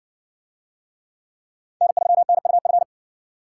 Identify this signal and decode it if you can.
Morse code, heard as a keyed tone, about 56 wpm, 700 Hz. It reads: N3NUF